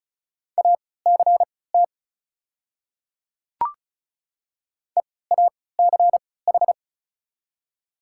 Morse code, heard as a keyed tone, about 36 wpm, spaced as for 20 wpm, 700 Hz. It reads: ACT EACH
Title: 3x Character Spacing - Top 200 Words - 35wpm